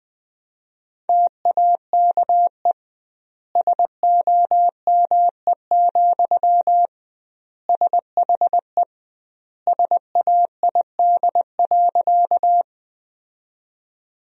Morse code, heard as a keyed tone, about 21 wpm, 700 Hz. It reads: TAKE SOME, SHE SAID.
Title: 2x Word-Spacing - Sentences from Top 200 Words - 20wpm